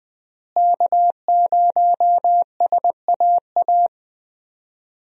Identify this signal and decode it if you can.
Morse code, heard as a keyed tone, about 20 wpm, 700 Hz. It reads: K0SAA
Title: USA Call Signs - 5 Characters - Repeated 3x - 20wpm